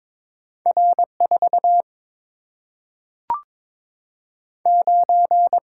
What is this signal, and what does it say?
Morse code, heard as a keyed tone, about 22 wpm, 700 Hz. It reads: R 4 9
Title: Single-Letter Single-Number 22wpm